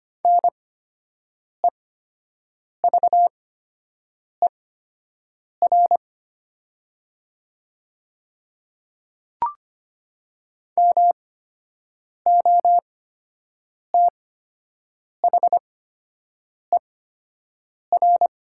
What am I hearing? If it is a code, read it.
Morse code, heard as a keyed tone, about 25 wpm, spaced as for 7 wpm, 700 Hz. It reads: NEVER MOTHER
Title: 8x Character Spacing - Top 500 Words - 25wpm